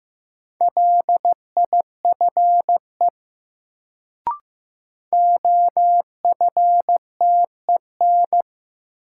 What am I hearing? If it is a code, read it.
Morse code, heard as a keyed tone, about 15 wpm, 700 Hz. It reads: LIFE OFTEN
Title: Sentences from Top 300 Words - New Word Drill - 15wpm